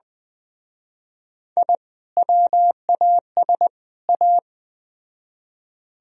Morse code, heard as a keyed tone, about 21 wpm, 700 Hz. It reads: I WAS A